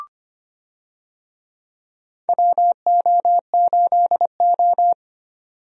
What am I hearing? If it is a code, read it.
Morse code, heard as a keyed tone, about 25 wpm, 700 Hz. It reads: WO8O
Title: USA Call Signs - 4 Characters - 25wpm